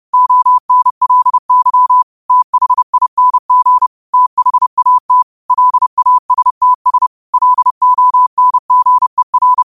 Morse code, heard as a keyed tone, about 30 wpm, 1000 Hz. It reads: ONLY THING THAT LASTS LONGER
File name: qotdfiles / 30wpm